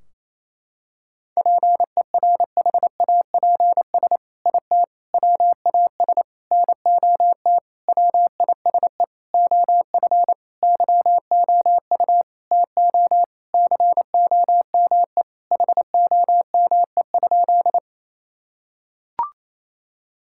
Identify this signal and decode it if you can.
Morse code, heard as a keyed tone, about 28 wpm, 700 Hz. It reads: PERHAPS IT WAS NOT WISE OF YOU TO COME HOME?
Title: Sentences from Top 1600 Words 28wpm